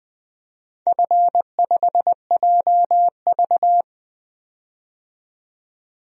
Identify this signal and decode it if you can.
Morse code, heard as a keyed tone, about 20 wpm, 700 Hz. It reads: F5JV
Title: DXCC-100 Call Signs - 4 Characters - 20wpm